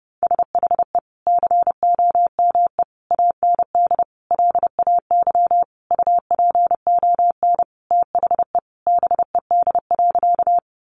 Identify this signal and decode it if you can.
Morse code, heard as a keyed tone, about 30 wpm, 700 Hz. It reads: SHE COME AND LAY UPON THE BED.
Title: Sentences from Top 1100 Words 30wpm